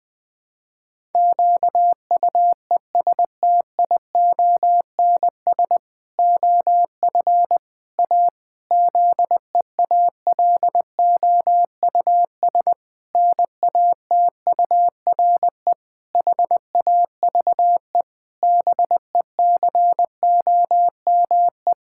Morse code, heard as a keyed tone, about 21 wpm, 700 Hz. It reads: QUESTIONS OF A ZEALOUS NATURE HAVE BECOME
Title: Pangram Challenge v2 - 20wpm